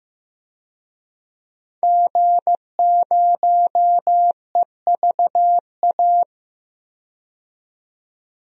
Morse code, heard as a keyed tone, about 15 wpm, 700 Hz. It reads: G0EVA
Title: DXCC-100 Call Signs - 5 Characters - 15wpm